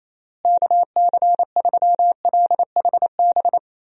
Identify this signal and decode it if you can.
Morse code, heard as a keyed tone, about 27 wpm, 700 Hz. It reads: KC3LHB
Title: USA Call Signs - 6 Characters - Repeated 3x - 28wpm